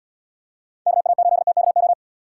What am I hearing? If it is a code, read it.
Morse code, heard as a keyed tone, about 76 wpm, 700 Hz. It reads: KA8TKO